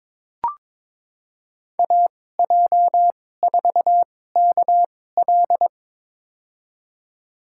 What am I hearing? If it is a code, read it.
Morse code, heard as a keyed tone, about 21 wpm, spaced as for 16 wpm, 700 Hz. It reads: AJ4KL
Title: USA Call Signs - 5 Characters - 2x Character Spacing - 22wpm